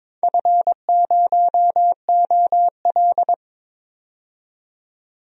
Morse code, heard as a keyed tone, about 22 wpm, 700 Hz. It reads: F0OL
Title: DXCC-100 Call Signs - 4 Characters - 22wpm